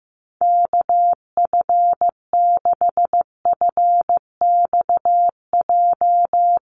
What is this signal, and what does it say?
Morse code, heard as a keyed tone, about 15 wpm, 700 Hz. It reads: KF6FXJ
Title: USA Call Signs 15wpm